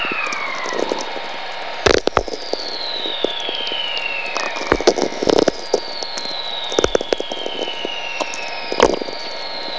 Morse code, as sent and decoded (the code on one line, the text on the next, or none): none